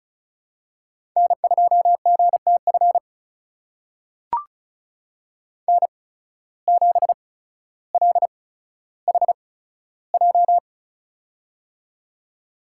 Morse code, heard as a keyed tone, about 36 wpm, spaced as for 15 wpm, 700 Hz. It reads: N2GTF N7LHJ
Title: USA Call Signs - 5 Characters - 8x Character Spacing - 35wpm